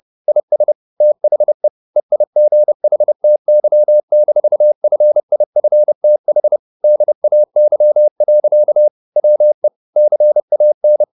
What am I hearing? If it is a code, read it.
Morse code, heard as a keyed tone, about 30 wpm, 600 Hz. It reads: IS THE EIGHTY-FIFTH DAY. WE CAN
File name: the old man and the sea / the old man and the sea 1 30wpm